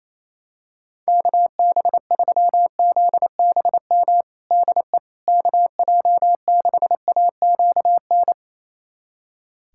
Morse code, heard as a keyed tone, about 28 wpm, 700 Hz. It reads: KB3ZBM DE KJ6AQN
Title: USA Callsign DE Callsign - 6 Characters - 28wpm